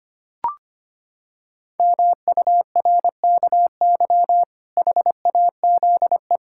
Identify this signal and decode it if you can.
Morse code, heard as a keyed tone, about 25 wpm, 700 Hz. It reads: MURKY HAZE
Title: Pangram Challenge v2 - 25wpm